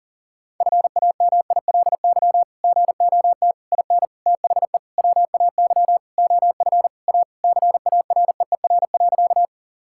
Morse code, heard as a keyed tone, about 40 wpm, 700 Hz. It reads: FAMILY GOT IN THE WAY OF A CAREER.
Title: Sentences from Top 1000 Words - Review Part 6 - 40wpm